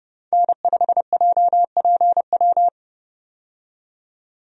Morse code, heard as a keyed tone, about 30 wpm, 700 Hz. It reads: N5JPW
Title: USA Call Signs - 5 Characters - 30wpm